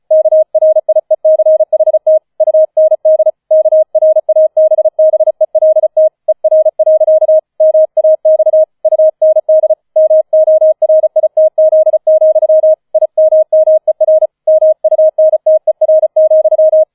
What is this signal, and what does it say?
Morse code, heard as a keyed tone, about 34 wpm, 630 Hz. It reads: KRIECHT UND KRABBELT ER. MAX UND MORITZ, IMMER MUNTER,
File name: 005 Fünfter Streich 35wpm